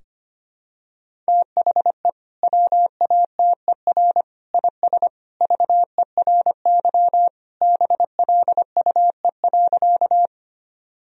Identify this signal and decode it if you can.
Morse code, heard as a keyed tone, about 25 wpm, 700 Hz. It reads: THE WATER IS VERY BLUE.
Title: Sentences from Top 1000 Words - Review Part 3 - 25wpm